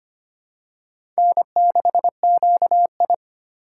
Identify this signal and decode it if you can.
Morse code, heard as a keyed tone, about 25 wpm, 700 Hz. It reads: N6QI
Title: USA Call Signs - 4 Characters - 2x Character Spacing - 25wpm